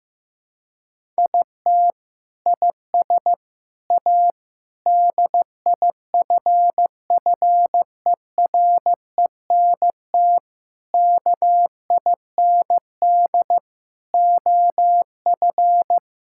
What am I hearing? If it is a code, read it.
Morse code, heard as a keyed tone, about 15 wpm, 700 Hz. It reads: IT IS A DIFFERENT KIND OF
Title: Sentences from Top 1000 Words - Review Part 5 - 15wpm